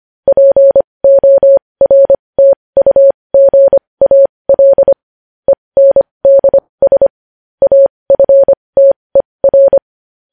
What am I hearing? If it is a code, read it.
Morse code, heard as a keyed tone, about 25 wpm, spaced as for 20 wpm, 550 Hz. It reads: PORTUGAL ENDS AFTER
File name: podcast / News headlines-1x-25x20